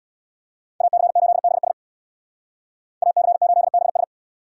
Morse code, heard as a keyed tone, about 60 wpm, 700 Hz. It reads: AC8BS AC8BS
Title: Speed Racing - USA Call Signs 50wpm